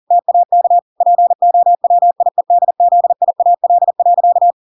Morse code, heard as a keyed tone, about 40 wpm, 700 Hz. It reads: TAK POWIEDZIAL.